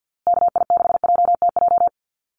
Morse code, heard as a keyed tone, about 50 wpm, 700 Hz. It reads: KI6JTJ
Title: USA Call Signs Repeated 2x 50wpm